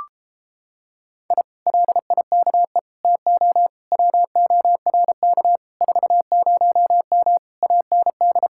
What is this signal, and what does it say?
Morse code, heard as a keyed tone, about 31 wpm, 700 Hz. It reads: I LIKE TO WORK 40M AND